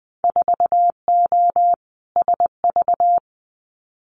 Morse code, heard as a keyed tone, about 20 wpm, 700 Hz. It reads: 4O SV
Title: Winter Field Day Exchanges 20wpm